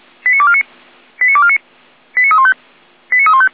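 Five-tone selective call (ZVEI), heard as quick runs of five tones, tones of about 70 ms. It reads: 89249 89249 89315 89315
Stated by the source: ZVEI five-tone selective call 89305, received by SDR.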